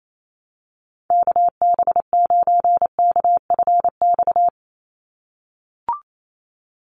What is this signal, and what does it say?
Morse code, heard as a keyed tone, about 28 wpm, 700 Hz. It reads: KB9KFX
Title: USA Call Signs 28wpm